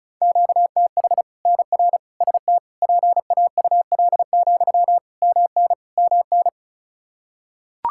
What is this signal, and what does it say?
Morse code, heard as a keyed tone, about 36 wpm, 700 Hz. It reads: QTH NR ST PAUL, MN MN
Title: Practice QSO - Exchange 1 - Beginning - 35wpm